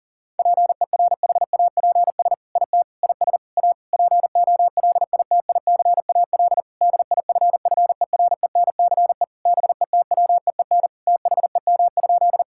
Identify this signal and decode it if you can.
Morse code, heard as a keyed tone, about 40 wpm, 700 Hz. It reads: PERHAPS IT IS A POLITICAL DIFFERENCE BETWEEN THEM?
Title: Sentences from Top 1000 Words - Review Part 5 - 40wpm